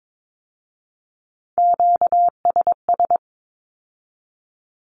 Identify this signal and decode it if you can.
Morse code, heard as a keyed tone, about 22 wpm, 700 Hz. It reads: QSS
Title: Q-Codes with Spoken Meaning - 22wpm